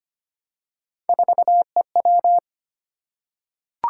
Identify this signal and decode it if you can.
Morse code, heard as a keyed tone, about 25 wpm, 700 Hz. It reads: 4EW
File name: Three Letter-Number Combinations - 25wpm